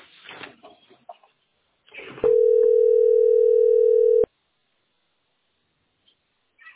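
Ringback tone (North America), 440 + 480 Hz.
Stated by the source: Telephone call recording from the tesspb.mdpbx PBX.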